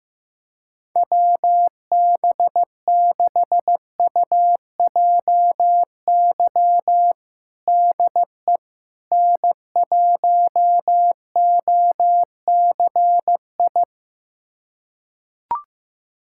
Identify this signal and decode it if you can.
Morse code, heard as a keyed tone, about 15 wpm, 700 Hz. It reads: WB6UJY DE N1OCI
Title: USA Callsign DE Callsign 15wpm